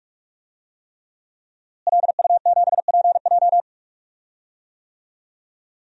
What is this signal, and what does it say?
Morse code, heard as a keyed tone, about 46 wpm, 700 Hz. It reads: LU7PJ
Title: DXCC-100 Call Signs - 5 Characters - Repeated 3x - 45wpm